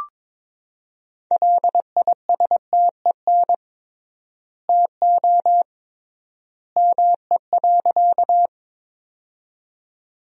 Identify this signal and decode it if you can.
Morse code, heard as a keyed tone, about 22 wpm, 700 Hz. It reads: LISTEN TO ME.
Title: 3x Word-Spacing - Sentences from Top 400 Words - 22wpm